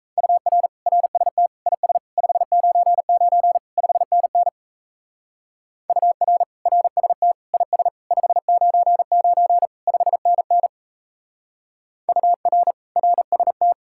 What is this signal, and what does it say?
Morse code, heard as a keyed tone, about 35 wpm, 700 Hz. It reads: UR RST IS 599 5NN UR RST IS 599 5NN UR RST
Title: Speed-Racing Common QSO Phrases Encore - 28wpm